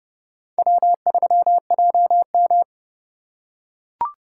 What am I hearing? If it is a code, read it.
Morse code, heard as a keyed tone, about 28 wpm, 700 Hz. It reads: W3JM